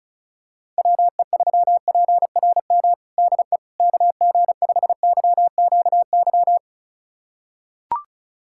Morse code, heard as a keyed tone, about 36 wpm, 700 Hz. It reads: WE3PRM DE KG5YQY
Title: USA Callsign DE Callsign - 6 Characters - 35wpm